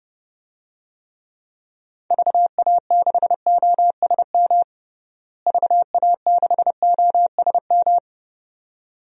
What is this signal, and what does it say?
Morse code, heard as a keyed tone, about 30 wpm, 700 Hz. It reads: VA6OSM VA6OSM
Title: DXCC-100 Call Signs - 6 Characters - Repeated 2x - 30wpm